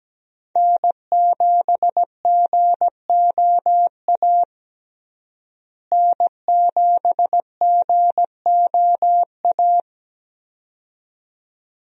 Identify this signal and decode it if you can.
Morse code, heard as a keyed tone, about 17 wpm, 700 Hz. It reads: N7GOA N7GOA
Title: USA Call Signs - 5 Characters - Repeated 2x - 17wpm